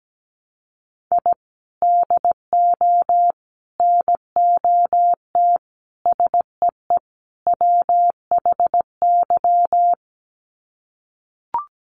Morse code, heard as a keyed tone, about 17 wpm, 700 Hz. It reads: I DO NOT SEE WHY
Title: Sets of 5 Words Encore 17wpm